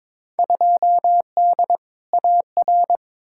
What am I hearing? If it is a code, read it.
Morse code, heard as a keyed tone, about 22 wpm, 700 Hz. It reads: 2D AR
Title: Field Day Exchanges Encore - 22wpm